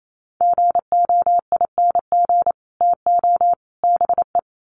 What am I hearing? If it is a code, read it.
Morse code, heard as a keyed tone, about 28 wpm, 700 Hz. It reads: GOING TO BE
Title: Sets of 3 Words 28wpm